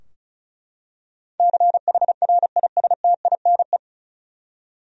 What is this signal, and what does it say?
Morse code, heard as a keyed tone, about 34 wpm, 700 Hz. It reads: CHRISTINE
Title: USA Common Names - 35wpm